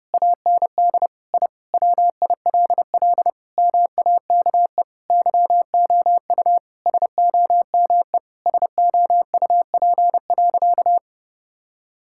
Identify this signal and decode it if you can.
Morse code, heard as a keyed tone, about 32 wpm, 700 Hz. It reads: AND I WILL MAKE YOU SOME SOUP.
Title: Sentences from Top 1000 Words - Review Part 6 - 30wpm